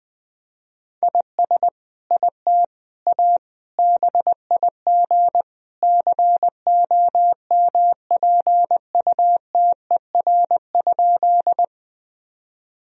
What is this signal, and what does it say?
Morse code, heard as a keyed tone, about 20 wpm, 700 Hz. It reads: IS IT A BIG COMPUTER?
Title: Sentences from Top 1000 Words - Review Part 3 - 20wpm